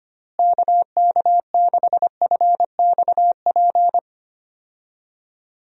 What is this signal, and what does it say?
Morse code, heard as a keyed tone, about 25 wpm, 700 Hz. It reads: KK6FXP
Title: USA Call Signs - 6 Characters - Repeated 2x - 25wpm